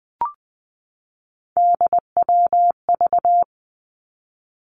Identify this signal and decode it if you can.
Morse code, heard as a keyed tone, about 20 wpm, 700 Hz. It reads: DWV
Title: Three Challenging Letter-Number Combinations 20wpm